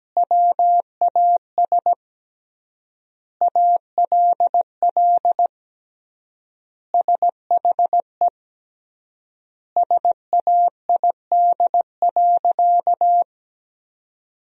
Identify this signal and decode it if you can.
Morse code, heard as a keyed tone, about 17 wpm, 700 Hz. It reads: WAS ALL SHE SAID.
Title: 3x Word-Spacing - Sentences from Top 100 Words - 17wpm